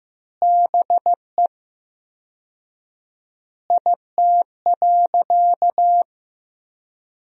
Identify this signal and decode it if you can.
Morse code, heard as a keyed tone, about 15 wpm, 700 Hz. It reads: BE IT.
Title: 4x Word-Spacing - Sentences from Top 100 Words - 15wpm